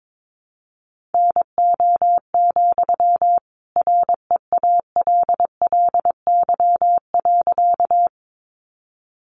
Morse code, heard as a keyed tone, about 22 wpm, 700 Hz. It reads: NO, REALLY.
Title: Sentences from Top 600 Words 22wpm